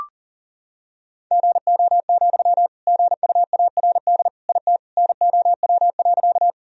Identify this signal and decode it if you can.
Morse code, heard as a keyed tone, about 40 wpm, 700 Hz. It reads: GO, GUARD IT NOW.
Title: Sentences from Top 1000 Words - Review Part 4 - 40wpm